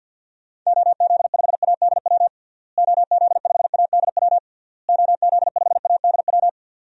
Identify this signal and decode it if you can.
Morse code, heard as a keyed tone, about 51 wpm, 700 Hz. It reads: OZ5ADW OZ5ADW OZ5ADW
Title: DXCC-100 Call Signs - 6 Characters - Repeated 3x - 50wpm